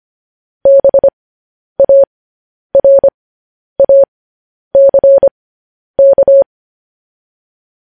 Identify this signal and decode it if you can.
Morse code, heard as a keyed tone, about 25 wpm, spaced as for 10 wpm, 550 Hz. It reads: BARACK
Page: news headlines in morse code at 10 wpm